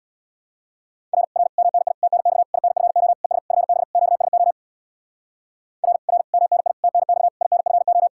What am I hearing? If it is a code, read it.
Morse code, heard as a keyed tone, about 62 wpm, 700 Hz. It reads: R R MNI TNX INFO ES FB QSO R R MNI TNX INFO
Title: Speed-Racing Common QSO Phrases - 50wpm